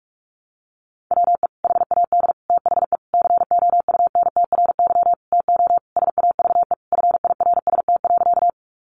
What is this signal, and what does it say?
Morse code, heard as a keyed tone, about 44 wpm, 700 Hz. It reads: WE HAD THE COUNTRY TO SAVE FIRST.
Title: Sentences from Top 1100 Words 45wpm